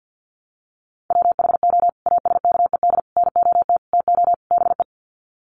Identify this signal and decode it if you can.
Morse code, heard as a keyed tone, about 50 wpm, 700 Hz. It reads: WHO ASKED NOT TO BE
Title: Sets of 5 Words Encore2 50wpm